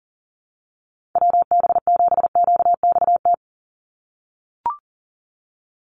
Morse code, heard as a keyed tone, about 40 wpm, 700 Hz. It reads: WB7QXT
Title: USA Call Signs 40wpm